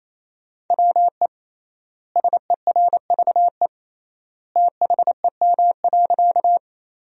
Morse code, heard as a keyed tone, about 28 wpm, 700 Hz. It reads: WE SERVE THEM.
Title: 3x Word-Spacing - Sentences from Top 400 Words - 28wpm